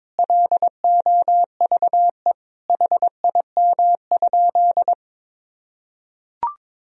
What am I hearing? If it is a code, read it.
Morse code, heard as a keyed tone, about 22 wpm, 700 Hz. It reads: LOVE HIM?
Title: Sentences from Top 400 Words and Abbreviations 22wpm